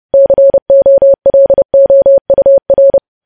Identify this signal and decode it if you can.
Morse code, heard as a keyed tone, about 30 wpm, 550 Hz. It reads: COLOUR
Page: news headlines in morse code at 30 wpm